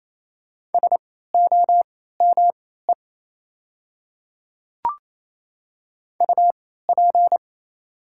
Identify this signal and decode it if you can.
Morse code, heard as a keyed tone, about 28 wpm, spaced as for 16 wpm, 700 Hz. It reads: SOME UP
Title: 3x Character Spacing - Top 100 Words - 28wpm